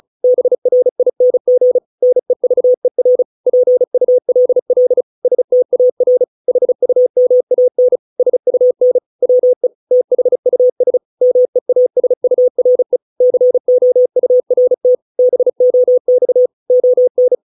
Morse code, heard as a keyed tone, about 34 wpm, 500 Hz. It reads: BRING NEVER PULL STAR HUMAN SUN WE THUS MEASURE COURT BOX ON